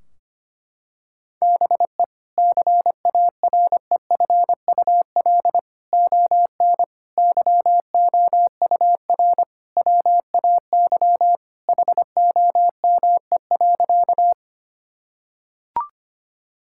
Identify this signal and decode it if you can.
Morse code, heard as a keyed tone, about 25 wpm, 700 Hz. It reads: BE CAREFUL ON YOUR WAY HOME.
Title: Sentences from Top 1600 Words 25wpm